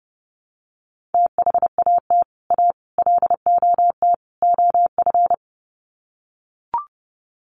Morse code, heard as a keyed tone, about 30 wpm, 700 Hz. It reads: THAT A LOT OF
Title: Sets of 4 Words 30wpm